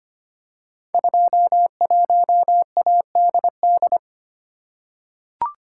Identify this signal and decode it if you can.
Morse code, heard as a keyed tone, about 25 wpm, 700 Hz. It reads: 21ADD
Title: Code Groups - 5 Letter:Numbers - 25wpm